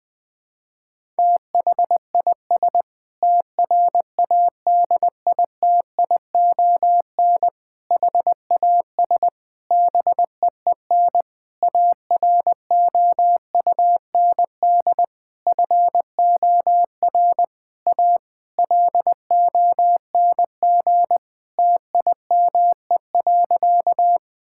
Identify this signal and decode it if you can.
Morse code, heard as a keyed tone, about 21 wpm, 700 Hz. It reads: THIS TRADITION HAS BEEN AROUND FOR A LONG TIME.
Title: Sentences with 7-Word Phrases - 20wpm